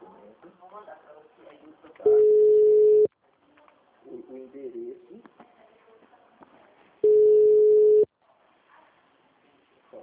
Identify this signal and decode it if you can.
Ringback tone (Europe, 425 Hz), with speech.